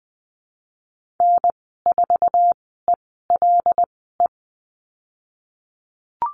Morse code, heard as a keyed tone, about 19 wpm, spaced as for 14 wpm, 700 Hz. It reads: N4ELE
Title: 088 Lesson 15g - Number 5 - Call Signs 20wpm